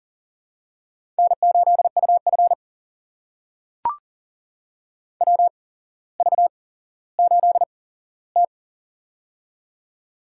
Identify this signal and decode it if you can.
Morse code, heard as a keyed tone, about 39 wpm, spaced as for 17 wpm, 700 Hz. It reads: N8UF WV8T
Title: USA Call Signs - 4 Characters - 8x Character Spacing - 40wpm